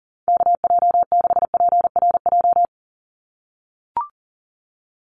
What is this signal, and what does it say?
Morse code, heard as a keyed tone, about 40 wpm, 690 Hz. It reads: KJ6PRJ
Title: USA Call Signs 40wpm